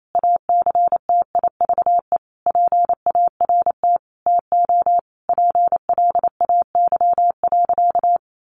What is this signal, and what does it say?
Morse code, heard as a keyed tone, about 28 wpm, 700 Hz. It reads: ACTIVE PART TO PLAY.
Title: Sentences from Top 900 Words 28wpm